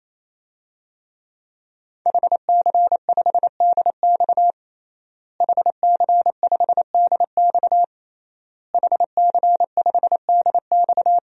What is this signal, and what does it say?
Morse code, heard as a keyed tone, about 28 wpm, 700 Hz. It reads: HC5DX HC5DX HC5DX
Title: DXCC-100 Call Signs - 5 Characters - Repeated 3x - 28wpm